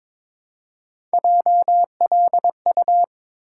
Morse code, heard as a keyed tone, about 22 wpm, 700 Hz. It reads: JLU
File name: practice / 22wpm / Three Letter-Number Combinations - 22wpm